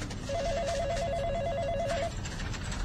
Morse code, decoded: none